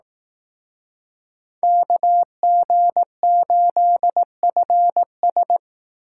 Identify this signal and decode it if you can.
Morse code, heard as a keyed tone, about 18 wpm, 700 Hz. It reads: KG8FS
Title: Speed Racing - USA 2x2 Call Signs 15wpm